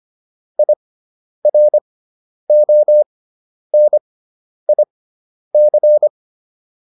Morse code, heard as a keyed tone, about 25 wpm, spaced as for 10 wpm, 600 Hz. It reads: IRONIC
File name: IR6 10